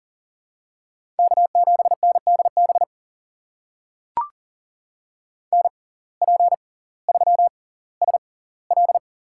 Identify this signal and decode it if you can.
Morse code, heard as a keyed tone, about 40 wpm, spaced as for 20 wpm, 700 Hz. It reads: K7NDB NP3SL
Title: USA Call Signs - 5 Characters - 6x Character Spacing - 40wpm